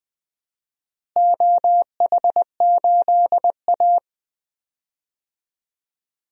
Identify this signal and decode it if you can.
Morse code, heard as a keyed tone, about 20 wpm, 700 Hz. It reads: OH8A
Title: DXCC-100 Call Signs - 4 Characters - Repeated 2x - 20wpm